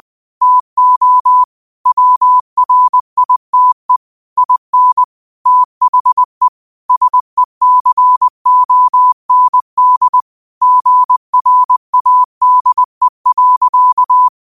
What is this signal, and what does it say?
Morse code, heard as a keyed tone, about 20 wpm, 1000 Hz. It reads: TO WRITE IN THE SECOND GRADE.